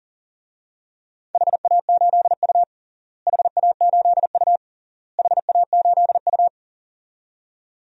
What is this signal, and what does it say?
Morse code, heard as a keyed tone, about 40 wpm, 700 Hz. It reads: HA8U HA8U HA8U
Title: DXCC-100 Call Signs - 4 Characters - Repeated 3x - 40wpm